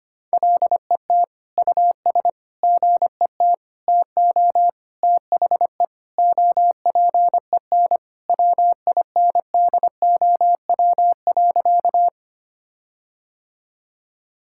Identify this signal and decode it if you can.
Morse code, heard as a keyed tone, about 25 wpm, 700 Hz. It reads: LET US GET TO THE OPEN WINDOW.
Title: Sentences from Top 1600 Words 25wpm